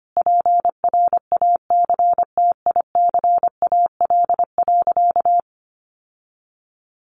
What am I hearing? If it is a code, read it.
Morse code, heard as a keyed tone, about 24 wpm, 700 Hz. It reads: PRACTICAL.
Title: Sentences from Top 1400 Words 25wpm